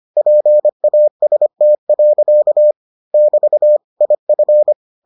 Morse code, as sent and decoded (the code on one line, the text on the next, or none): .--. .- ... - .-.-.- / -...- / .. ..-.
PAST. <BT> IF